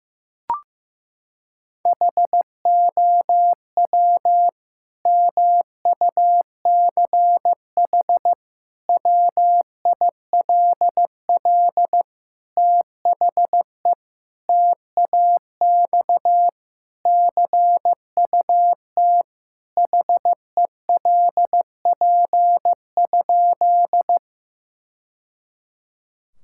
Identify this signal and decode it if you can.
Morse code, heard as a keyed tone, about 15 wpm, 700 Hz. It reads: HOW MUCH WILL THE TAX CUT HELP?
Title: Sentences from Top 1000 Words - Review Part 5 - 15wpm